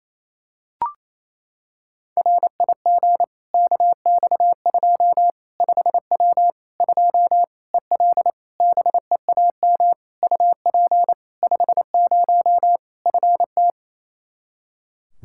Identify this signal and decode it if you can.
Morse code, heard as a keyed tone, about 28 wpm, 700 Hz. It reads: RIG KX2 5W 2 EL BEAM UP 50 FT
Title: Practice QSO - Exchange 2 - Info - 28wpm